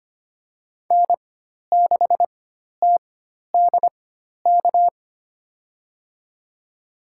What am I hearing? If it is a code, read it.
Morse code, heard as a keyed tone, about 25 wpm, spaced as for 12 wpm, 700 Hz. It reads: N6TDK
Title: USA Call Signs - 4x Character Spacing - 25wpm